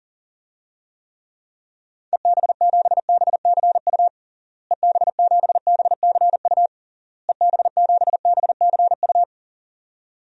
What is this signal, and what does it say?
Morse code, heard as a keyed tone, about 40 wpm, 700 Hz. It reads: EB7BCU EB7BCU EB7BCU
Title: DXCC-100 Call Signs - 6 Characters - Repeated 3x - 40wpm